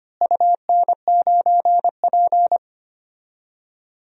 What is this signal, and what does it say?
Morse code, heard as a keyed tone, about 25 wpm, 700 Hz. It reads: UN9P
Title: DXCC-100 Call Signs - 4 Characters - Repeated 2x - 25wpm